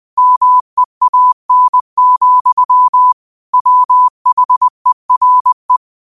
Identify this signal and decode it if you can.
Morse code, heard as a keyed tone, about 20 wpm, 1000 Hz. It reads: MEAN, WHERE
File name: qotdfiles / 20wpm